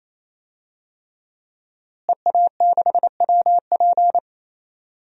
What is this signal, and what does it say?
Morse code, heard as a keyed tone, about 28 wpm, 700 Hz. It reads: EA6WP
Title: DXCC-100 Call Signs - 5 Characters - Repeated 3x - 28wpm